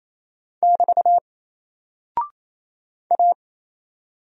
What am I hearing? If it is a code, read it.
Morse code, heard as a keyed tone, about 27 wpm, 700 Hz. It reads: <BT> A
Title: Single Character (Letter:Number:Punctuation:Prosign) 28wpm